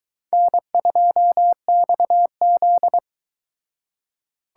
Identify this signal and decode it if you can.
Morse code, heard as a keyed tone, about 23 wpm, 700 Hz. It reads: N2XZ